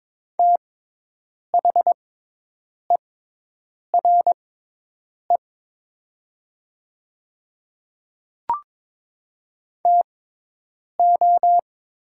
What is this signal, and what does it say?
Morse code, heard as a keyed tone, about 23 wpm, spaced as for 8 wpm, 700 Hz. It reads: THERE TO
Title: 6x Character Spacing - Top 200 Words - 22wpm